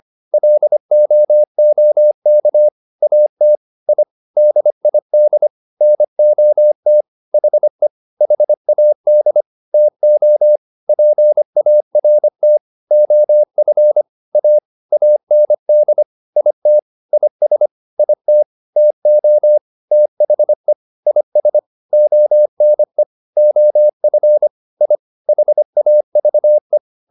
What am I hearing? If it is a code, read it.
Morse code, heard as a keyed tone, about 25 wpm, 600 Hz. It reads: LOOK AT I DID NOT HE HAD TO PART OF A AND IT IS IT TO THE IS ONE OF I HAVE